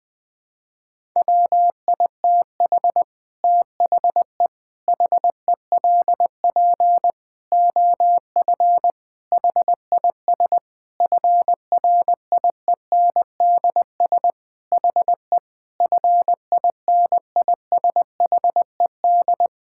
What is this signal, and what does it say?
Morse code, heard as a keyed tone, about 20 wpm, 700 Hz. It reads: WITH THE HELP OF HIS FRIENDS HE FINISHED
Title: Sentences with 4-Word Phrases - Spoken Beginning - 20wpm